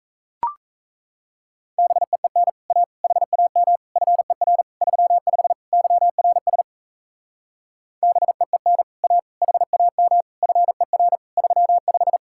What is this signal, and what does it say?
Morse code, heard as a keyed tone, about 36 wpm, 700 Hz. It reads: BEEN A HAM FER 35 YRS BEEN A HAM FER 35